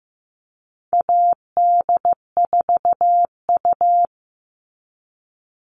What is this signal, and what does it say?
Morse code, heard as a keyed tone, about 15 wpm, 700 Hz. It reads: AD4U
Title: All Cut Numbers - 4-Digit 15wpm